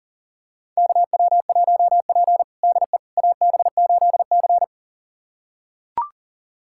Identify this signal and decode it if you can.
Morse code, heard as a keyed tone, about 37 wpm, 700 Hz. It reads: KW1P DE AB8C
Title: USA Callsign DE Callsign - 4 Characters - 40wpm